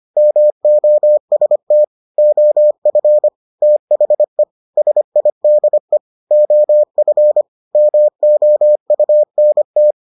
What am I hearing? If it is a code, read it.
Morse code, heard as a keyed tone, about 25 wpm, 600 Hz. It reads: MOST OF THE SIDE OF MOUNT